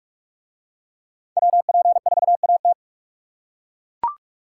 Morse code, heard as a keyed tone, about 42 wpm, 700 Hz. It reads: WP4AT